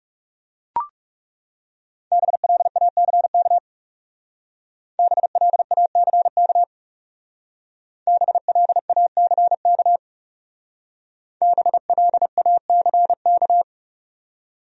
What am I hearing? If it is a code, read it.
Morse code, heard as a keyed tone, about 37 wpm, 700 Hz. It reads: BLACK BLACK BLACK BLACK
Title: Speed-Racing Top 500 Words 30wpm